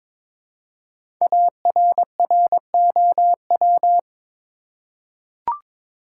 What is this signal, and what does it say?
Morse code, heard as a keyed tone, about 23 wpm, 700 Hz. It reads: ARROW
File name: 5 Letter Words - First 3 Letters Given - 22wpm